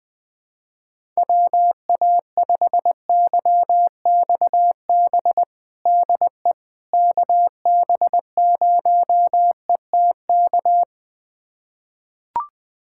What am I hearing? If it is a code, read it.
Morse code, heard as a keyed tone, about 20 wpm, 700 Hz. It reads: WA5YXB DE KB0ETK
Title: USA Callsign DE Callsign - 6 Characters - 20wpm